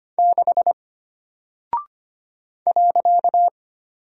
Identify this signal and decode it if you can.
Morse code, heard as a keyed tone, about 25 wpm, 700 Hz. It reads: 6 .
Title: Single Character (Letter:Number:Punctuation:Prosign) Alt 25wpm